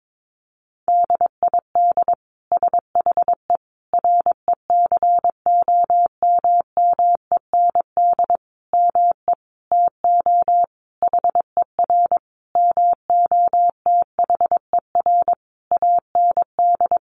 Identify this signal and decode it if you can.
Morse code, heard as a keyed tone, about 22 wpm, 700 Hz. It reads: DID SHE RECOMMEND ME TO HER MOTHER AND
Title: Sentences from Top 1200 Words 22wpm